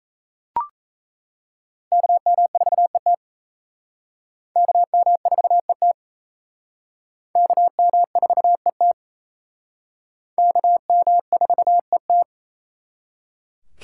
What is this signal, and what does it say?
Morse code, heard as a keyed tone, about 35 wpm, 700 Hz. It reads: KM4ET KM4ET KM4ET KM4ET